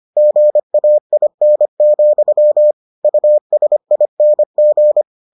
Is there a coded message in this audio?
Yes: Morse code from 0.2 to 5.0 s, heard as a keyed tone, about 25 wpm, 600 Hz. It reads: GAIN, USING